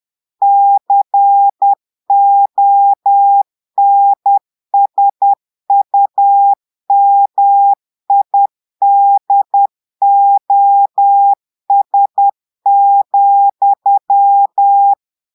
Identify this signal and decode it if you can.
Morse code, heard as a keyed tone, about 10 wpm, 800 Hz. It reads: CONSUMIDOS,
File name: portugais texte en clair 2 vitesse 10 mots minutes